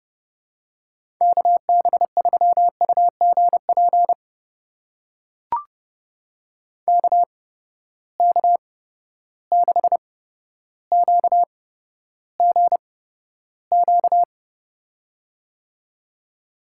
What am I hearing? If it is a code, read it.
Morse code, heard as a keyed tone, about 30 wpm, spaced as for 13 wpm, 700 Hz. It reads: KB3UGP KK6QGQ
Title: USA Call Signs - 6 Characters - 8x Character Spacing - 30wpm